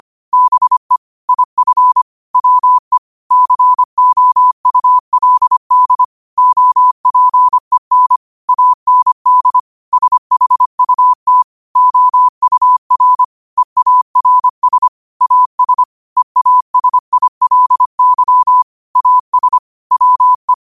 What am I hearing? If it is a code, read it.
Morse code, heard as a keyed tone, about 25 wpm, 1000 Hz. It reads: BE IF WE COULD OPEN AND SHUT OUR EARS AS EASILY AS WE
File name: qotdfiles / 25wpm